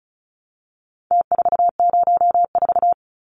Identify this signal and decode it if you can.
Morse code, heard as a keyed tone, about 34 wpm, 700 Hz. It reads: T404
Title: Numbers - Contest Style - 4-Digit 35wpm